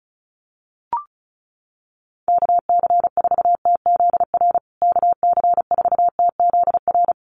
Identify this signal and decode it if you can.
Morse code, heard as a keyed tone, about 34 wpm, 700 Hz. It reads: KC4TZR KC4TZR
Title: USA Call Signs Repeated 3x 35wpm